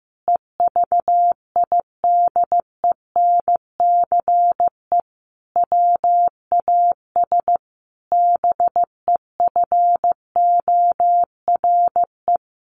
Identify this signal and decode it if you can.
Morse code, heard as a keyed tone, about 14 wpm, 700 Hz. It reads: EVIDENCE WAS BEFORE
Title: Sentences from Top 1300 Words 15wpm